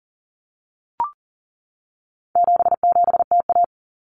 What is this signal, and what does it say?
Morse code, heard as a keyed tone, about 40 wpm, 700 Hz. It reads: 77TA
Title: Common Cut Numbers - 4-Digit 40wpm